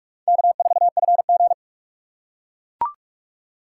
Morse code, heard as a keyed tone, about 42 wpm, 700 Hz. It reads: K4FG
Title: USA Call Signs - 4 Characters - 4x Character Spacing - 45wpm